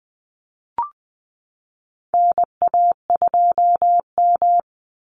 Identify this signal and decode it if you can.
Morse code, heard as a keyed tone, about 20 wpm, 700 Hz. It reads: NA2M
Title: Call Signs Repeated 2x - 20wpm